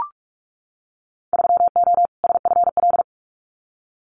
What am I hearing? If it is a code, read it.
Morse code, heard as a keyed tone, about 44 wpm, 700 Hz. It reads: 3O SFL
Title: Winter Field Day Exchanges 45wpm